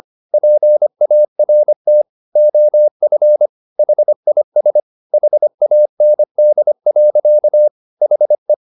Morse code, heard as a keyed tone, about 25 wpm, 600 Hz. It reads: PART OF HIS HAND. HE
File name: the old man and the sea part 1 25wpm